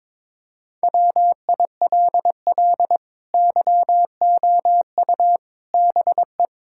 Morse code, heard as a keyed tone, about 21 wpm, 700 Hz. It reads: WILL YOU BE
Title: Sentences from Top 1600 Words 22wpm